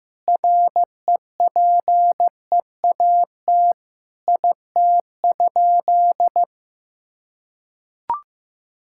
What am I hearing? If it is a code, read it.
Morse code, heard as a keyed tone, about 16 wpm, 700 Hz. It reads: REPEAT IT?
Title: Sentences from Top 1600 Words 15wpm